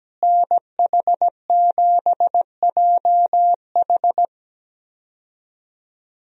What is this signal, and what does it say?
Morse code, heard as a keyed tone, about 17 wpm, 700 Hz. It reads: NH7JH
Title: USA Call Signs - 5 Characters - Repeated 2x - 17wpm